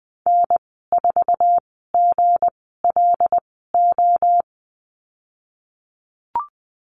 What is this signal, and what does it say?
Morse code, heard as a keyed tone, about 20 wpm, spaced as for 15 wpm, 700 Hz. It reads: N4GLO
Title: 173 Lesson 28g - Number 9 - Call Signs 20wpm